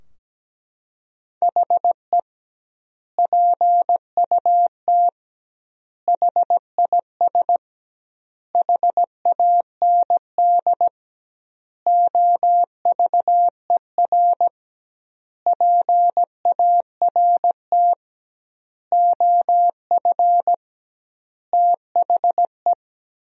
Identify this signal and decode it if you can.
Morse code, heard as a keyed tone, about 17 wpm, 700 Hz. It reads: HE PUT HIS HAND OVER PART OF THE
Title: 2x Word-Spacing - Sentences from Top 400 Words - 17wpm